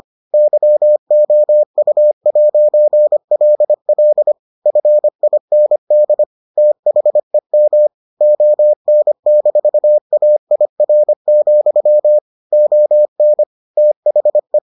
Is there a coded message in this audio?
Yes: Morse code from 0.3 to 14.6 s, heard as a keyed tone, about 25 wpm, 600 Hz. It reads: YOU'LL FIND THEM ON-AIR, ON THE